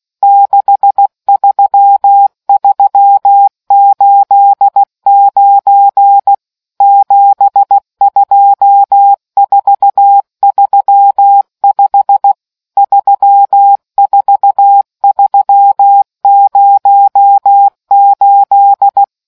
Morse code, heard as a keyed tone, about 16 wpm, 800 Hz. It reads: 63389 72435 34308